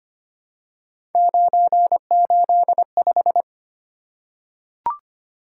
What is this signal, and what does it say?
Morse code, heard as a keyed tone, about 25 wpm, 700 Hz. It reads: 985
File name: Three Letter-Number Combinations - 25wpm